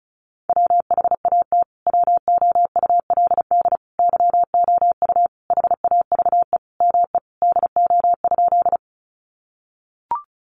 Morse code, heard as a keyed tone, about 34 wpm, 700 Hz. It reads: WHAT WOULD YOU HAVE ME DO?
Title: Sentences from Top 200 Words 35wpm